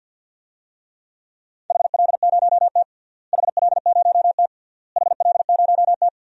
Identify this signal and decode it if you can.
Morse code, heard as a keyed tone, about 53 wpm, 700 Hz. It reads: HL0T HL0T HL0T
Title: DXCC-100 Call Signs - 4 Characters - Repeated 3x - 50wpm